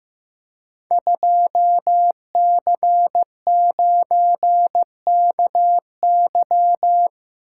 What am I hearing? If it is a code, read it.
Morse code, heard as a keyed tone, about 15 wpm, 700 Hz. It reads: 2C9KY
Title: Code Groups - 5 Letter:Numbers - 15wpm